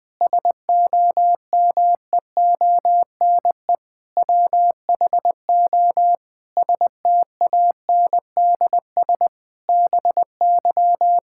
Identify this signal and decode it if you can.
Morse code, heard as a keyed tone, about 20 wpm, 700 Hz. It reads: SOMEONE WHO STANDS BY
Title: Difficult Sentences with Hints - 20wpm